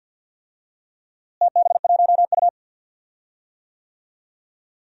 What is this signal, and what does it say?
Morse code, heard as a keyed tone, about 47 wpm, 700 Hz. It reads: TB1U
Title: DXCC-100 Call Signs - 4 Characters - Repeated 2x - 50wpm